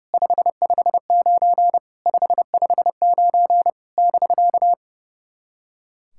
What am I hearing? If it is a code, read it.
Morse code, heard as a keyed tone, about 30 wpm, 700 Hz. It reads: 559 559 <BK>
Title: SOTA QSO Practice 30wpm